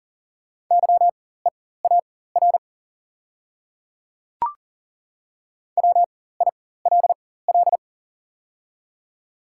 Morse code, heard as a keyed tone, about 40 wpm, spaced as for 19 wpm, 690 Hz. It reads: YEAR WILL
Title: 4x Character Spacing - Top 500 Words - 40wpm